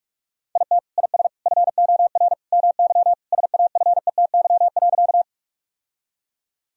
Morse code, heard as a keyed tone, about 41 wpm, 700 Hz. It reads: IT IS FOR MY SAFETY.
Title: Sentences from Top 1000 Words - Review Part 4 - 45wpm